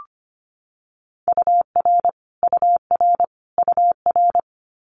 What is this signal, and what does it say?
Morse code, heard as a keyed tone, about 25 wpm, 700 Hz. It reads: UR UR UR
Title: Abbreviations Repeated 3x - 25wpm